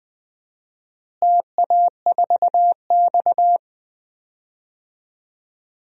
Morse code, heard as a keyed tone, about 20 wpm, 700 Hz. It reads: TA4X
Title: DXCC-100 Call Signs - 4 Characters - Repeated 3x - 20wpm